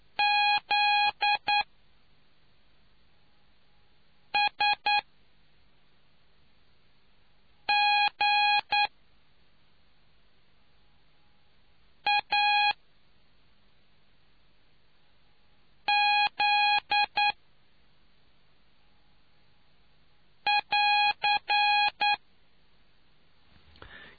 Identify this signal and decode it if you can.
Morse code, heard as a keyed tone, about 9 wpm, spaced as for 3 wpm, 830 Hz. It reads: ZSGAZ <AR>